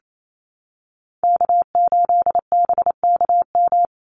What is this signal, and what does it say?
Morse code, heard as a keyed tone, about 28 wpm, 700 Hz. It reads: K8BKM
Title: Call Signs Repeated 3x - 28wpm